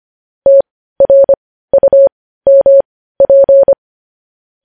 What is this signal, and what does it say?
Morse code, heard as a keyed tone, about 25 wpm, spaced as for 15 wpm, 550 Hz. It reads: TRUMP
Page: News Headlines in Morse Code at 15 WPM